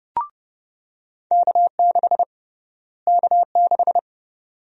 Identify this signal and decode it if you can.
Morse code, heard as a keyed tone, about 30 wpm, 700 Hz. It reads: K6 K6
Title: USA Call Sign Prefixes - Repeated 2x - 30wpm